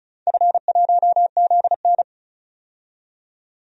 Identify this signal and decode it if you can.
Morse code, heard as a keyed tone, about 34 wpm, 700 Hz. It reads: F1ZN
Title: DXCC-100 Call Signs - 4 Characters - 35wpm